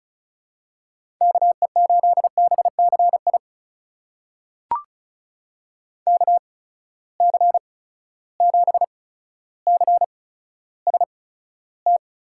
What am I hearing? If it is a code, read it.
Morse code, heard as a keyed tone, about 36 wpm, spaced as for 15 wpm, 700 Hz. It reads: KE8BCI KC7CST
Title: USA Call Signs - 6 Characters - 8x Character Spacing - 35wpm